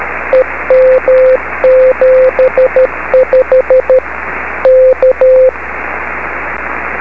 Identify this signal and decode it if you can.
Morse code, heard as a keyed tone, about 13 wpm, 520 Hz, with radio noise. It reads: EM75 K
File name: beacons / k4kwk-b